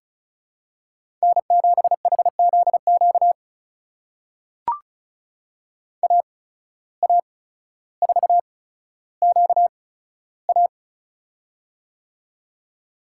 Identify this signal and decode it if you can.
Morse code, heard as a keyed tone, about 34 wpm, spaced as for 15 wpm, 700 Hz. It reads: N7HZQ AA4QA
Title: USA Call Signs - 5 Characters - 8x Character Spacing - 35wpm